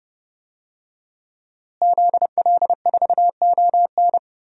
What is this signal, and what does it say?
Morse code, heard as a keyed tone, about 30 wpm, 700 Hz. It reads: ZL4ON
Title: DXCC-100 Call Signs - 5 Characters - Repeated 3x - 30wpm